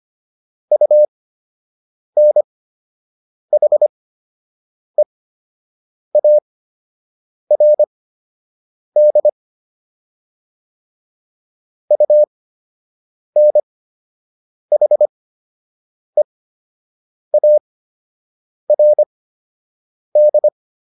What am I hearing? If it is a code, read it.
Morse code, heard as a keyed tone, about 25 wpm, spaced as for 7 wpm, 600 Hz. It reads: UNHEARD UNHEARD